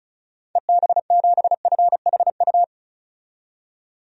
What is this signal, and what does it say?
Morse code, heard as a keyed tone, about 34 wpm, 700 Hz. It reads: EB7FHU
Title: DXCC-100 Call Signs - 6 Characters - 35wpm